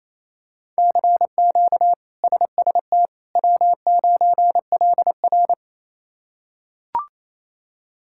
Morse code, heard as a keyed tone, about 28 wpm, 700 Hz. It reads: CQ SST W9LR
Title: K1USN SST QSO Practice - 28wpm